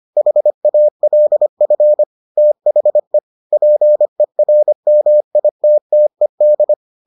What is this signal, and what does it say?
Morse code, heard as a keyed tone, about 25 wpm, 600 Hz. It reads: HALF THE PERMITTED